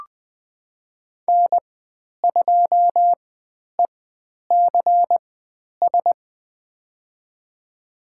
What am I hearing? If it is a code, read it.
Morse code, heard as a keyed tone, about 20 wpm, spaced as for 10 wpm, 700 Hz. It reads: N2ECS
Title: 109 Lesson 18g - Letter m - Call Signs 20wpm